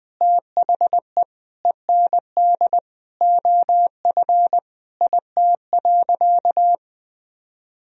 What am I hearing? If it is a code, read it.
Morse code, heard as a keyed tone, about 21 wpm, 700 Hz. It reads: THE END OF IT.
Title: Sentences from Top 200 Words and Abbreviations 20wpm